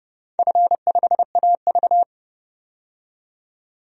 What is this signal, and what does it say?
Morse code, heard as a keyed tone, about 32 wpm, 700 Hz. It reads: F5AV
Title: DXCC-100 Call Signs - 4 Characters - 30wpm